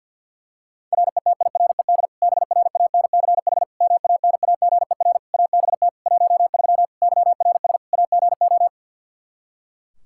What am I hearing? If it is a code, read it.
Morse code, heard as a keyed tone, about 50 wpm, 700 Hz. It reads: RETIRED BRANCH MANAGER ABT 13 YRS AGO
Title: K1USN SST QSO Practice 50wpm